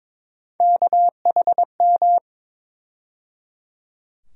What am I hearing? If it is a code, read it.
Morse code, heard as a keyed tone, about 22 wpm, 700 Hz. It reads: KHM
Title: USA Call Sign Postfixes - 3 Letters 22wpm